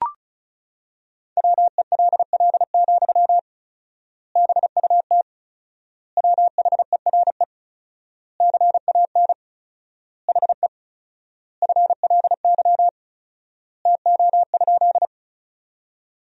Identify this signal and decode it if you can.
Morse code, heard as a keyed tone, about 36 wpm, 700 Hz. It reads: WELL, BUT WHERE CAN HE FLY TO?
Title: 4x Word-Spacing - Sentences from Top 500 Words - 35wpm